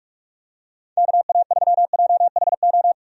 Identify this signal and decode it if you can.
Morse code, heard as a keyed tone, about 46 wpm, 700 Hz. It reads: KA3JHO
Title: USA Call Signs - 6 Characters - 45wpm